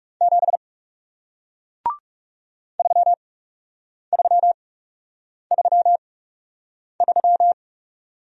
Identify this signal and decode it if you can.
Morse code, heard as a keyed tone, about 40 wpm, spaced as for 8 wpm, 700 Hz. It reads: 7 3333